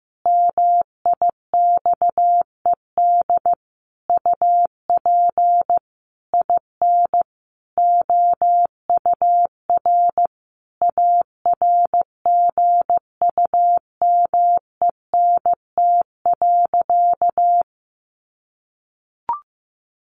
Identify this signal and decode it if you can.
Morse code, heard as a keyed tone, about 15 wpm, 700 Hz. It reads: MIXED UP IN OUR ARGUMENT.
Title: Sentences from Top 1500 Words 15wpm